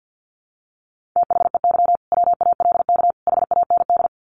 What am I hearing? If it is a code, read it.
Morse code, heard as a keyed tone, about 50 wpm, 700 Hz. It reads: THEY WALK HAND
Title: Sentences from Top 1400 Words 50wpm